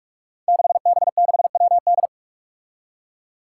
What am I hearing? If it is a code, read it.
Morse code, heard as a keyed tone, about 46 wpm, 700 Hz. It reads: 6B6WD
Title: Five Challenging Letter-Number Combinations 45wpm